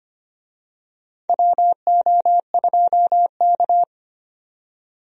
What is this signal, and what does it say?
Morse code, heard as a keyed tone, about 25 wpm, 700 Hz. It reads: WO2K
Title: USA Call Signs - 4 Characters - 25wpm